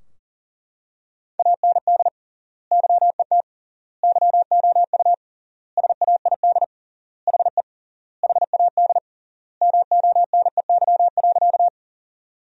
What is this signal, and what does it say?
Morse code, heard as a keyed tone, about 40 wpm, 700 Hz. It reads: AND YET YOU SAID HE HAD MONEY.
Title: 3x Word-Spacing - Sentences from Top 500 Words - 40wpm